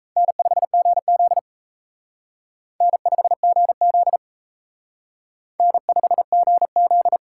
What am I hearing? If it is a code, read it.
Morse code, heard as a keyed tone, about 35 wpm, 700 Hz. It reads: N5GZ N5GZ N5GZ